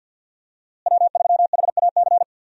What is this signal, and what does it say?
Morse code, heard as a keyed tone, about 51 wpm, 700 Hz. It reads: W3HAC
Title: USA Call Signs - 5 Characters - Repeated 3x - 50wpm